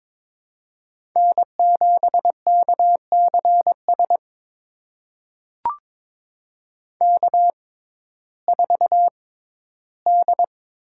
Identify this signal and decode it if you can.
Morse code, heard as a keyed tone, about 22 wpm, spaced as for 13 wpm, 700 Hz. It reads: N7KCS K4D
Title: USA Call Signs - 5 Characters - 6x Character Spacing - 22wpm